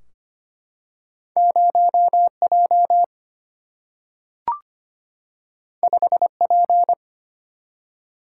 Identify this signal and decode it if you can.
Morse code, heard as a keyed tone, about 25 wpm, 700 Hz. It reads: 0J 5P
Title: Single-Letter Single-Number 25wpm